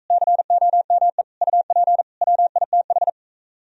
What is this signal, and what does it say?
Morse code, heard as a keyed tone, about 44 wpm, 690 Hz. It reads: COME UP WITH